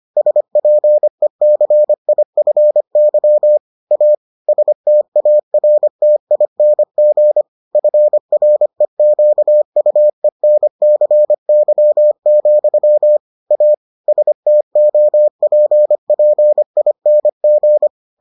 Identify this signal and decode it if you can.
Morse code, heard as a keyed tone, about 25 wpm, 600 Hz. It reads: SPECIFY A STARTING FREQUENCY, A STOPPING